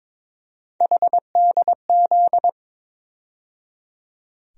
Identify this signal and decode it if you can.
Morse code, heard as a keyed tone, about 22 wpm, 700 Hz. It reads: HDZ
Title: USA Call Sign Postfixes - 3 Letters 22wpm